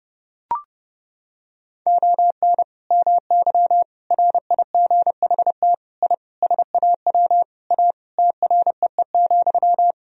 Morse code, heard as a keyed tone, about 32 wpm, 700 Hz. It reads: ON MY RIGHT I SAW A TREE,
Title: Sentences from Top 1000 Words - Review Part 2 - 30wpm